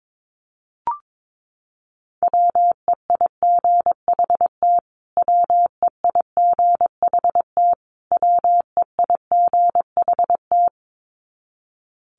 Morse code, heard as a keyed tone, about 22 wpm, 700 Hz. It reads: WEIGHT WEIGHT WEIGHT
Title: Top 500 Words Repeated 22wpm